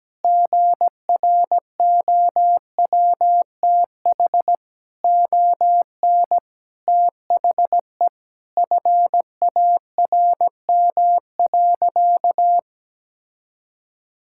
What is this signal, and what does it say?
Morse code, heard as a keyed tone, about 17 wpm, 700 Hz. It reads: GROWTH ON THE FARM.
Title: Sentences from Top 1000 Words - Review Part 5 - 17wpm